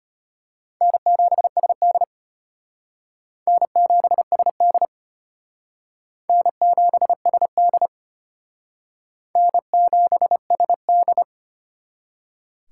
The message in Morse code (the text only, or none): N7SD N7SD N7SD N7SD